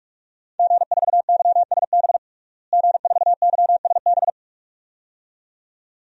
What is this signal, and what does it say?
Morse code, heard as a keyed tone, about 46 wpm, 700 Hz. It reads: G4YSB G4YSB
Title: DXCC-100 Call Signs - 5 Characters - Repeated 2x - 45wpm